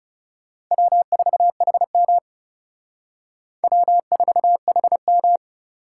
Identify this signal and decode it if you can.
Morse code, heard as a keyed tone, about 37 wpm, 700 Hz. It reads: W4HM W4HM
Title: Speed Racing - USA Call Signs 30wpm